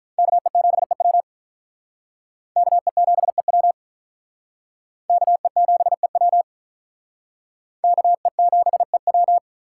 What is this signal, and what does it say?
Morse code, heard as a keyed tone, about 46 wpm, 700 Hz. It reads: KE7EW KE7EW KE7EW KE7EW